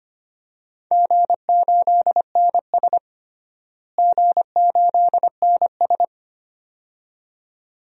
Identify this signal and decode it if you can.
Morse code, heard as a keyed tone, about 25 wpm, 700 Hz. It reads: G8NS G8NS
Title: DXCC-100 Call Signs - 4 Characters - Repeated 2x - 25wpm